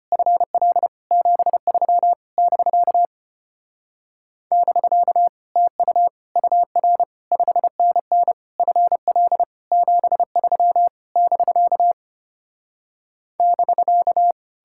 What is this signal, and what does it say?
Morse code, heard as a keyed tone, about 33 wpm, 700 Hz. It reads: FL 73 <BK> <BK> TU UR 5NN FL 73 <BK> <BK>